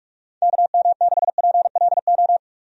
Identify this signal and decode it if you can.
Morse code, heard as a keyed tone, about 46 wpm, 700 Hz. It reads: KM6PLO
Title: USA Call Signs - 6 Characters - 45wpm